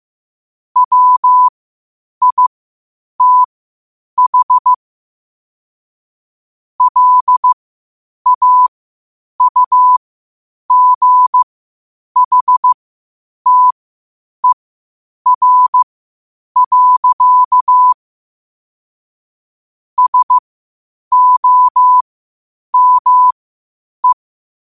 Morse code, heard as a keyed tone, about 15 wpm, spaced as for 8 wpm, 1000 Hz. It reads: WITH LAUGHTER. SOME
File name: qotdfiles / 8wpm